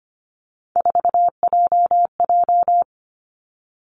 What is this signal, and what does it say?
Morse code, heard as a keyed tone, about 25 wpm, 700 Hz. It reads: 4JJ
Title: Three Challenging Letter-Number Combinations 25wpm